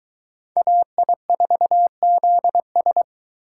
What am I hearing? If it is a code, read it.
Morse code, heard as a keyed tone, about 24 wpm, 700 Hz. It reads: AI4ZS